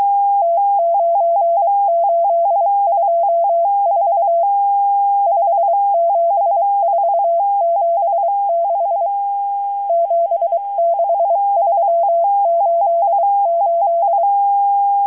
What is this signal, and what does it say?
Morse code, heard as a keyed tone, about 23 wpm, 680 Hz. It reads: T9824 57476 76388
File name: HM02- -0437utc-7351khz